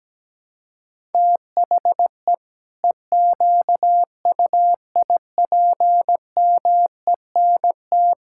Morse code, heard as a keyed tone, about 17 wpm, 700 Hz. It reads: THE EQUIPMENT
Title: Sentences from Top 1000 Words - Review Part 5 - 17wpm